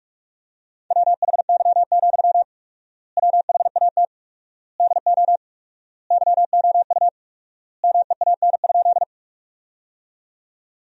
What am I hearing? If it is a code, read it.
Morse code, heard as a keyed tone, about 47 wpm, 700 Hz. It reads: WHY, WHAT DO YOU MEAN?
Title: 4x Word-Spacing - Sentences from Top 200 Words - 45wpm